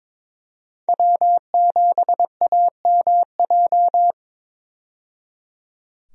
Morse code, heard as a keyed tone, about 22 wpm, 700 Hz. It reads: W7AMJ
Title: USA Call Signs - 5 Characters - 22wpm